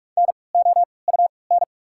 Morse code, heard as a keyed tone, about 46 wpm, spaced as for 26 wpm, 700 Hz. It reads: NOUN
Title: 3x Character Spacing - Top 500 Words - 45wpm